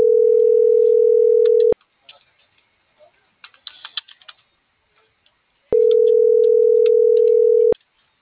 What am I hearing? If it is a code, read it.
Ringback tone (North America), 440 + 480 Hz.